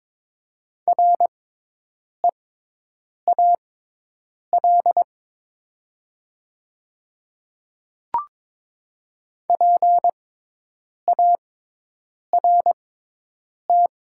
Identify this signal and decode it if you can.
Morse code, heard as a keyed tone, about 22 wpm, spaced as for 8 wpm, 700 Hz. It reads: REAL PART
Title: 6x Character Spacing - Top 300 Words - 22wpm